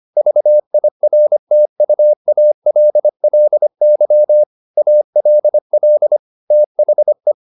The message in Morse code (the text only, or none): VIRTUALLY ALL THE